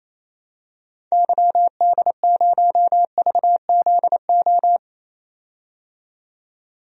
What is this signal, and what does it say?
Morse code, heard as a keyed tone, about 28 wpm, 700 Hz. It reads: YD0VZO
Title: DXCC-100 Call Signs - 6 Characters - 28wpm